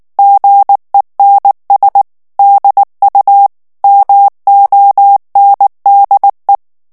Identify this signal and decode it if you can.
Morse code, heard as a keyed tone, about 19 wpm, 800 Hz. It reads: GENS DU MONDE
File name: FRANCAIS TEXTE CLAIR 2 VITESSE 18 MOTS MINUTES